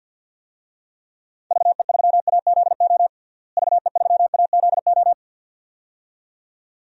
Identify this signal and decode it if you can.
Morse code, heard as a keyed tone, about 51 wpm, 700 Hz. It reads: VE3AZO VE3AZO
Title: DXCC-100 Call Signs - 6 Characters - Repeated 2x - 50wpm